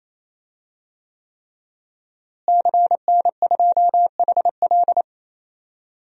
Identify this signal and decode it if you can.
Morse code, heard as a keyed tone, about 27 wpm, 700 Hz. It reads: CN2HL
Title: DXCC-100 Call Signs - 5 Characters - Repeated 3x - 28wpm